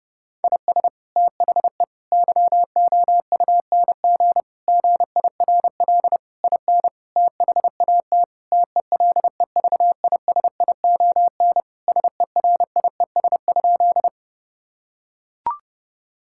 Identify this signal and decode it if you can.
Morse code, heard as a keyed tone, about 30 wpm, 700 Hz. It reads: IS THE YOUNG GIRL IN THAT TELEVISION SERIES?
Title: Sentences from Top 1000 Words - Review Part 5 - 30wpm